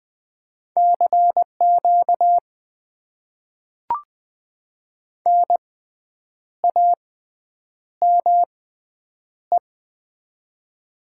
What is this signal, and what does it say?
Morse code, heard as a keyed tone, about 21 wpm, spaced as for 8 wpm, 700 Hz. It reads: CQ NAME
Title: Top 90 QSO Elements - 6x Character Spacing - 20wpm